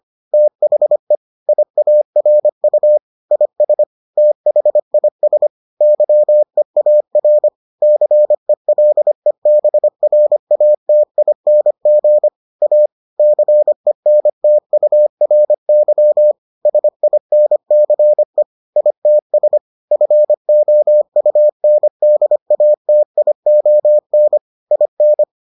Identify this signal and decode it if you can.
Morse code, heard as a keyed tone, about 25 wpm, 600 Hz. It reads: THE IARU IS THIS YEAR CELEBRATING A CENTURY SINCE ITS FOUNDATION IN